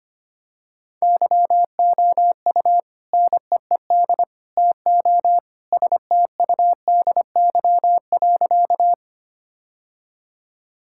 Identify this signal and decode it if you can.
Morse code, heard as a keyed tone, about 25 wpm, 700 Hz. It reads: YOU NEED TO STUDY.
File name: practice / 25wpm / Sentences with 3-Word Phrases - 25wpm